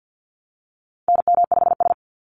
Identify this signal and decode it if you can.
Morse code, heard as a keyed tone, about 50 wpm, 700 Hz. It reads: NM5S
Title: NA SOTA Activator Callsigns 50wpm